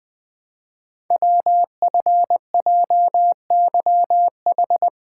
Morse code, heard as a keyed tone, about 20 wpm, 700 Hz. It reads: WFJYH
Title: Code Groups - 5 Letters - 20wpm